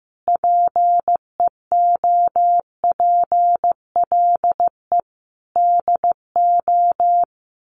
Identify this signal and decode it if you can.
Morse code, heard as a keyed tone, about 15 wpm, 700 Hz. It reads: PEOPLE DO
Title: Sets of 5 Words 15wpm